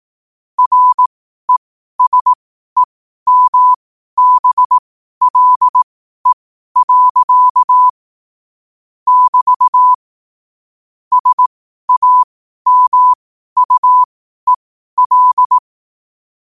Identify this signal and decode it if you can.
Morse code, heard as a keyed tone, about 18 wpm, spaced as for 12 wpm, 1000 Hz. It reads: RESEMBLE. <BT> SAMUEL
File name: qotdfiles / 12wpm